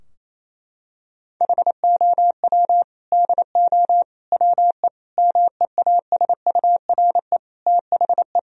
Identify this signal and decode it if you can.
Morse code, heard as a keyed tone, about 28 wpm, 700 Hz. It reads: HOW DO WE MEASURE THE
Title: Sentences from Top 1000 Words - Review Part 3 - 28wpm